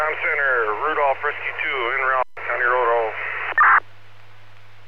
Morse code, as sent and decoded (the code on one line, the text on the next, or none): none